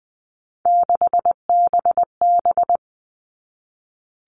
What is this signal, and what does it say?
Morse code, heard as a keyed tone, about 20 wpm, 700 Hz. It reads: 6BB
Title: All Cut Numbers - 3-Digit 20wpm